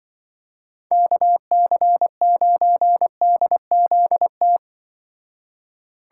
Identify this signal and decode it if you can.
Morse code, heard as a keyed tone, about 23 wpm, 700 Hz. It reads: KC9DZT
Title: Speed Racing - USA 2x3 Call Signs 20wpm